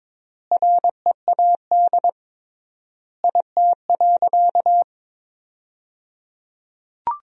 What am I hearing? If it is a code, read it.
Morse code, heard as a keyed tone, about 22 wpm, 700 Hz. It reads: READ IT.
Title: 3x Word-Spacing - Sentences from Top 500 Words - 22wpm